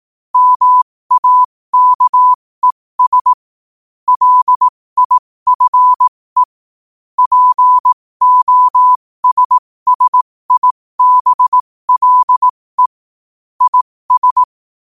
Morse code, heard as a keyed tone, about 18 wpm, spaced as for 15 wpm, 1000 Hz. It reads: MAKES LIFE POSSIBLE IS
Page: qotd-at-15-wpm